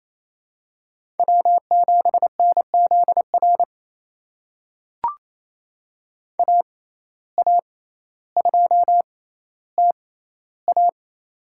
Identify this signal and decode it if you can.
Morse code, heard as a keyed tone, about 28 wpm, spaced as for 14 wpm, 700 Hz. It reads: W7NZR AA2TA
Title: USA Call Signs - 5 Characters - 6x Character Spacing - 28wpm